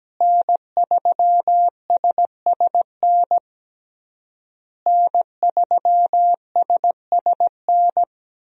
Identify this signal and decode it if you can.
Morse code, heard as a keyed tone, about 17 wpm, 700 Hz. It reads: N3SSN N3SSN
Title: USA Call Signs - 5 Characters - Repeated 3x - 17wpm